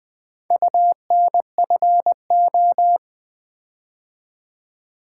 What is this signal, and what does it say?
Morse code, heard as a keyed tone, about 20 wpm, 700 Hz. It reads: UNFO
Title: Build A Word - 4 Letters At A Time - 11-Letter Words - 20wpm